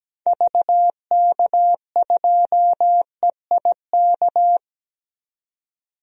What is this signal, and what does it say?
Morse code, heard as a keyed tone, about 17 wpm, 700 Hz. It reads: VK2EIK
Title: DXCC-100 Call Signs - 6 Characters - 17wpm